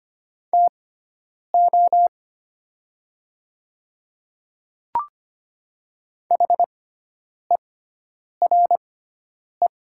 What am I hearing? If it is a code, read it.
Morse code, heard as a keyed tone, about 25 wpm, spaced as for 9 wpm, 690 Hz. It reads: TO HERE
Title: 6x Character Spacing - Top 200 Words - 25wpm